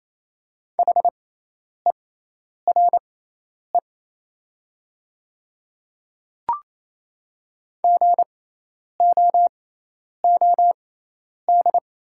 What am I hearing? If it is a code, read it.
Morse code, heard as a keyed tone, about 28 wpm, spaced as for 10 wpm, 700 Hz. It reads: HERE GOOD
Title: 6x Character Spacing - Top 300 Words - 28wpm